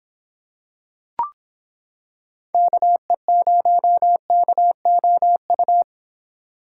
Morse code, heard as a keyed tone, about 26 wpm, 700 Hz. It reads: KE0KOU